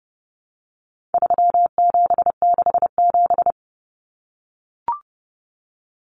Morse code, heard as a keyed tone, about 30 wpm, 700 Hz. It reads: 3767
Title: Numbers - 4-Digit 30wpm